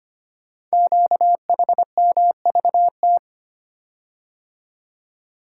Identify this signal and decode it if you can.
Morse code, heard as a keyed tone, about 25 wpm, 700 Hz. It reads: QHMVT
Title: Code Groups - 5 Letters - 25wpm